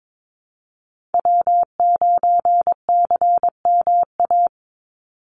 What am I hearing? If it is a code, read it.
Morse code, heard as a keyed tone, about 22 wpm, 700 Hz. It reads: W9CMA
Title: USA Call Signs 22wpm